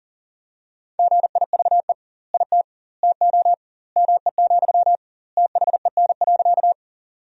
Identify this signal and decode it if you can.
Morse code, heard as a keyed tone, about 40 wpm, 700 Hz. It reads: GIVE IT TO ME, THEN.
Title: 2x Word-Spacing - Sentences from Top 200 Words - 40wpm